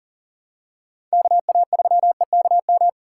Morse code, heard as a keyed tone, about 40 wpm, 700 Hz. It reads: KA3EKM
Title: USA Call Signs - 6 Characters - 2x Character Spacing - 40wpm